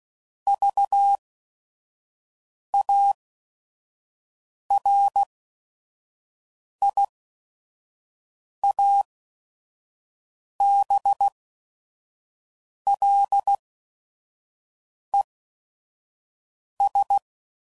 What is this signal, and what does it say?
Morse code, heard as a keyed tone, about 16 wpm, spaced as for 5 wpm, 800 Hz. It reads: VARIABLES
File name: CW Code Groups @5WPM / Technical Subjects / AND Gate